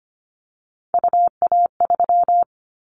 Morse code, heard as a keyed tone, about 25 wpm, 700 Hz. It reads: UA3
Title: Callsign Prefixes for Top 100 DXCC - 25wpm